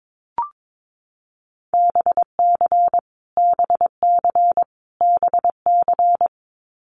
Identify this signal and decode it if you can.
Morse code, heard as a keyed tone, about 22 wpm, 700 Hz. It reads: BC BC BC
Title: Canadian Provinces and Territories - Repeated 3x - 22wpm